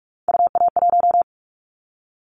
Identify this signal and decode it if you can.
Morse code, heard as a keyed tone, about 44 wpm, 700 Hz. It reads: UA1
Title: Callsign Prefixes for Top 100 DXCC - 45wpm